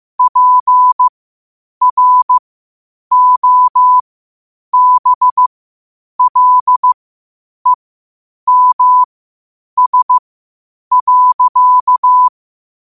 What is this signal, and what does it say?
Morse code, heard as a keyed tone, about 15 wpm, spaced as for 8 wpm, 1000 Hz. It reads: PROBLEMS.
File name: qotdfiles / 8wpm